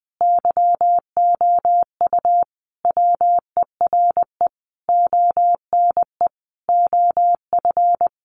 Morse code, heard as a keyed tone, about 19 wpm, 700 Hz. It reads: YOU WERE ONE OF
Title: Sentences from Top 1500 Words 20wpm